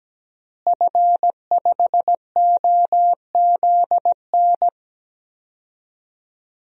Morse code, heard as a keyed tone, about 17 wpm, 700 Hz. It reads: F5OZN
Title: DXCC-100 Call Signs - 5 Characters - Repeated 3x - 17wpm